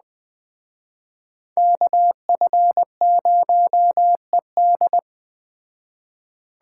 Morse code, heard as a keyed tone, about 20 wpm, 700 Hz. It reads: KF0ED
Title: Speed Racing - USA 2x2 Call Signs 17wpm